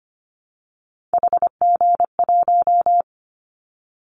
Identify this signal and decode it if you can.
Morse code, heard as a keyed tone, about 25 wpm, 700 Hz. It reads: HG1
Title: Callsign Prefixes for Top 50 DXCC - 25wpm